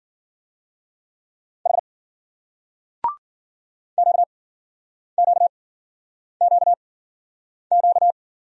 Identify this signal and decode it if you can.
Morse code, heard as a keyed tone, about 56 wpm, spaced as for 9 wpm, 700 Hz. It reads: H QQQQ